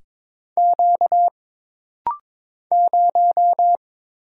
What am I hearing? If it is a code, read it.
Morse code, heard as a keyed tone, about 22 wpm, 700 Hz. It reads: Q 0
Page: This rapid-fire timing is geared to push your ICR (Instant Character Recognition) skills.